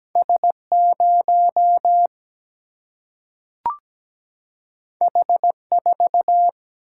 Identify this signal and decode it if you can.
Morse code, heard as a keyed tone, about 17 wpm, 700 Hz. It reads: S0 H4
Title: Single-Letter Single-Number 17wpm